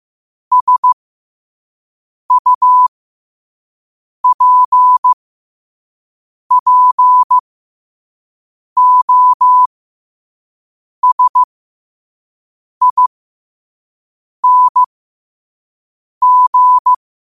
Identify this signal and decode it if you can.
Morse code, heard as a keyed tone, about 15 wpm, spaced as for 5 wpm, 1000 Hz. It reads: SUPPOSING
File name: qotdfiles / 5wpm